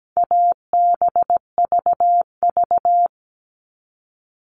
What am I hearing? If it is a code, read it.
Morse code, heard as a keyed tone, about 17 wpm, 700 Hz. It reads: ABVV
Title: All Cut Numbers - 4-Digit 17wpm